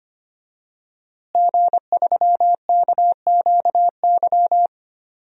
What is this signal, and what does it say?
Morse code, heard as a keyed tone, about 25 wpm, 700 Hz. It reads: G3KQY
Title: DXCC-100 Call Signs - 5 Characters - Repeated 3x - 25wpm